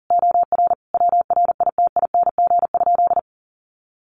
Morse code, heard as a keyed tone, about 40 wpm, 700 Hz. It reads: OR WRITING?
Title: Sentences from Top 1200 Words 40wpm